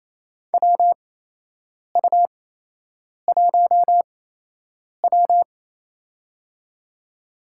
Morse code, heard as a keyed tone, about 28 wpm, spaced as for 8 wpm, 700 Hz. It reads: WU1W
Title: USA Call Signs - 4 Characters - 8x Character Spacing - 28wpm